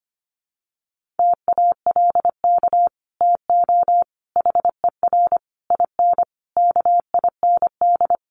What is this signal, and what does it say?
Morse code, heard as a keyed tone, about 25 wpm, 700 Hz. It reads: TALK TO HER IN KIND
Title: Sentences from Top 1500 Words 25wpm